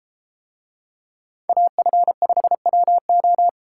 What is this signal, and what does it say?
Morse code, heard as a keyed tone, about 33 wpm, 700 Hz. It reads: AF5WO